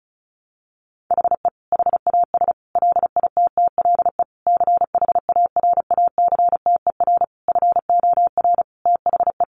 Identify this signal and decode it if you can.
Morse code, heard as a keyed tone, about 34 wpm, 700 Hz. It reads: HE HAS LITTLE CHARACTER FOR THE
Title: Sentences from Top 1200 Words 35wpm